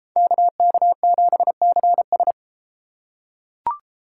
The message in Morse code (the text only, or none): KK7CS